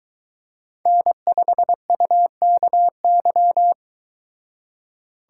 Morse code, heard as a keyed tone, about 22 wpm, 700 Hz. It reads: N5UKY